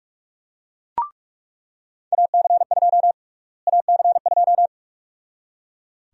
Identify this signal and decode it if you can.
Morse code, heard as a keyed tone, about 41 wpm, 700 Hz. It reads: AC2 AC2
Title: USA Call Sign Prefixes - Repeated 2x - 45wpm